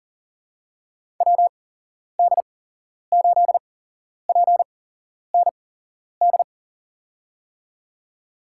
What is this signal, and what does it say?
Morse code, heard as a keyed tone, about 40 wpm, spaced as for 11 wpm, 700 Hz. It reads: WD8PND
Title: USA Call Signs - 6 Characters - 8x Character Spacing - 40wpm